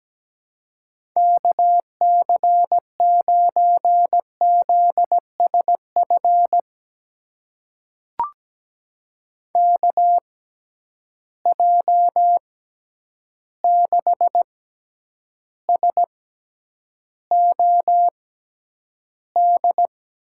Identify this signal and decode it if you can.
Morse code, heard as a keyed tone, about 17 wpm, spaced as for 9 wpm, 700 Hz. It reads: KC9ZSF KJ6SOD
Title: USA Call Signs - 6 Characters - 6x Character Spacing - 17wpm